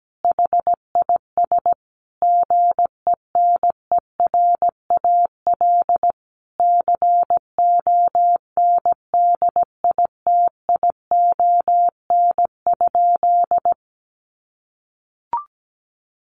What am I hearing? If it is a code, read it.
Morse code, heard as a keyed tone, about 17 wpm, 700 Hz. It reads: HIS GENERAL CONDITION?
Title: Sentences from Top 1000 Words 17wpm